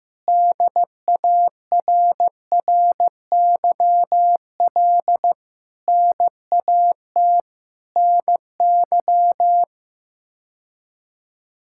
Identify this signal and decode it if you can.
Morse code, heard as a keyed tone, about 15 wpm, 700 Hz. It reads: DARRYL NAT NY
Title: K1USN SST QSO Practice - 15wpm